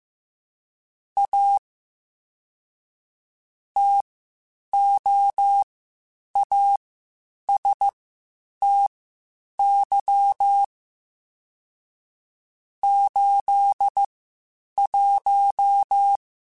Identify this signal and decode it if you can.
Morse code, heard as a keyed tone, about 15 wpm, spaced as for 8 wpm, 800 Hz. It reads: A TOASTY 81
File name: Spaced Characters / Spaced Characters-mp3s / Weather 5 12 sp @13